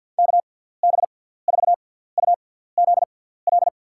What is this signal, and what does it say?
Morse code, heard as a keyed tone, about 51 wpm, spaced as for 17 wpm, 700 Hz. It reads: KB4UZL
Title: USA Call Signs - 6x Character Spacing - 50wpm